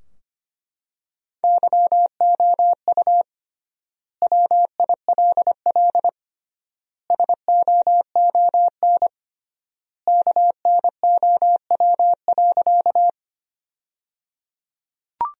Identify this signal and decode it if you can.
Morse code, heard as a keyed tone, about 25 wpm, 700 Hz. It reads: YOU WILL SOON KNOW.
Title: 3x Word-Spacing - Sentences from Top 400 Words - 25wpm